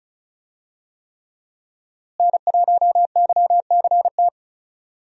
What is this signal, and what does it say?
Morse code, heard as a keyed tone, about 36 wpm, 700 Hz. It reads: N1YCT
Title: USA Call Signs - 5 Characters - Repeated 3x - 35wpm